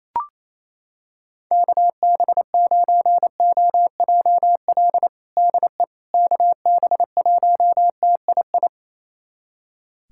Morse code, heard as a keyed tone, about 28 wpm, 700 Hz. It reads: KB9OJL DE KB1TII
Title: USA Callsign DE Callsign 28wpm